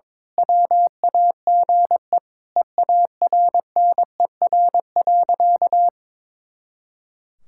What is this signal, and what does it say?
Morse code, heard as a keyed tone, about 22 wpm, 700 Hz. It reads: WAGE EARNER.
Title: Pangram Challenge v2 - 22wpm